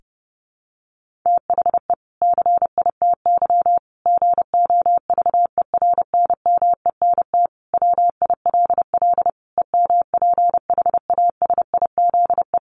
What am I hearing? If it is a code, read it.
Morse code, heard as a keyed tone, about 28 wpm, 700 Hz. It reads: THE CITY GOVERNMENT WILL EMPHASIZE
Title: Sentences from Top 1500 Words 30wpm